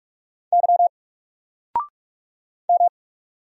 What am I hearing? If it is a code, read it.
Morse code, heard as a keyed tone, about 46 wpm, 700 Hz. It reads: Y M
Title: Single Letter-Number 45wpm